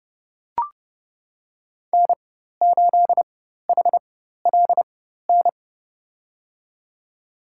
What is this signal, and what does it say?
Morse code, heard as a keyed tone, about 31 wpm, spaced as for 14 wpm, 700 Hz. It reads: N8HLN
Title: USA Call Signs - 4x Character Spacing - 30wpm